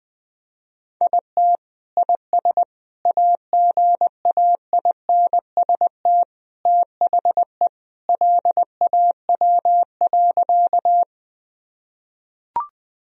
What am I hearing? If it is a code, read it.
Morse code, heard as a keyed tone, about 20 wpm, 700 Hz. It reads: IT IS AGAINST THE LAW.
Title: Sentences from Top 1000 Words - Review Part 3 - 20wpm